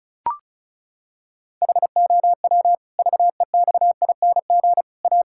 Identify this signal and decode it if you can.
Morse code, heard as a keyed tone, about 34 wpm, 700 Hz. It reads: HOW VEXING A
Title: Pangram Challenge v2 - 35wpm